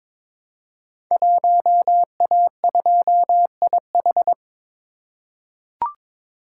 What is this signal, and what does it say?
Morse code, heard as a keyed tone, about 22 wpm, 700 Hz. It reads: 1A2IH
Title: Code Groups - 5 Letter:Numbers - 22wpm